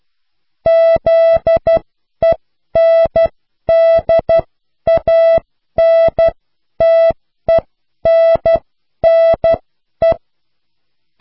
Morse code, heard as a keyed tone, about 12 wpm, spaced as for 10 wpm, 660 Hz. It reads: ZENDANTENNE